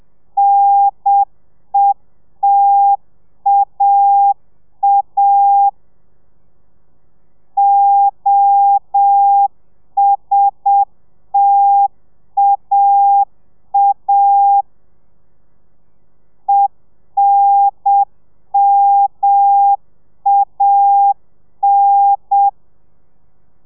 Morse code, heard as a keyed tone, about 7 wpm, 800 Hz. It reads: NETAA OSTAA ENMAN